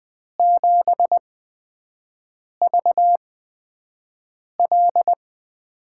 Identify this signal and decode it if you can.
Morse code, heard as a keyed tone, about 20 wpm, spaced as for 5 wpm, 700 Hz. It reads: 7VL
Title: USA Call Signs - 6 Characters - 8x Character Spacing - 20wpm